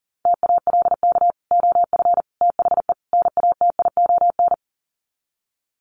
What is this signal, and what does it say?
Morse code, heard as a keyed tone, about 40 wpm, 700 Hz. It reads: TALK OF THE NATION
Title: Sets of 4 Words 40wpm